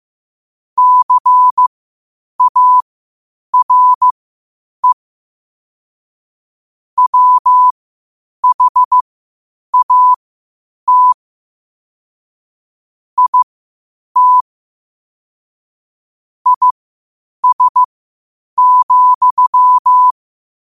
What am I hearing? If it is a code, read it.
Morse code, heard as a keyed tone, about 15 wpm, spaced as for 8 wpm, 1000 Hz. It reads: CARE WHAT IT IS,
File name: qotdfiles / 8wpm